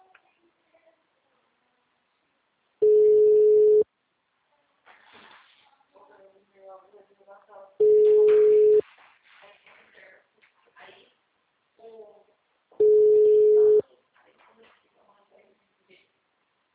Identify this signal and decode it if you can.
Ringback tone (Europe, 425 Hz), with speech.
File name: out- -2003- - - .595